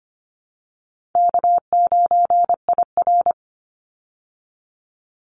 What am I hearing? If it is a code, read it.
Morse code, heard as a keyed tone, about 25 wpm, 700 Hz. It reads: K9IR
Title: Easy Call Signs Repeated 2x - 25wpm